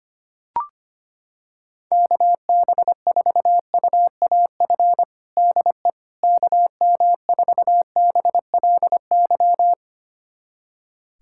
Morse code, heard as a keyed tone, about 25 wpm, 700 Hz. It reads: KB4UAF DE KM4BLY
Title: USA Callsign DE Callsign 25wpm